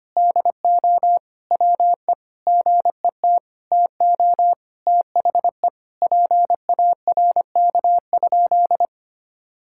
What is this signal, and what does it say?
Morse code, heard as a keyed tone, about 25 wpm, 700 Hz. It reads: DO WE GET TO THE PARK?
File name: practice / 25wpm / Sentences with 4-Word Phrases - 25wpm